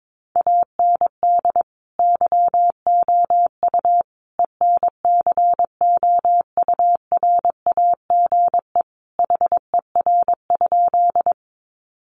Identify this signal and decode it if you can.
Morse code, heard as a keyed tone, about 22 wpm, 700 Hz. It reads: AND YOU ENCOURAGE HER?
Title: Sentences from Top 1400 Words 22wpm